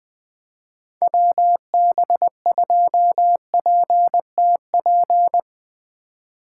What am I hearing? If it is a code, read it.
Morse code, heard as a keyed tone, about 21 wpm, 700 Hz. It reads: WB2PTP
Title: Speed Racing - USA 2x3 Call Signs 20wpm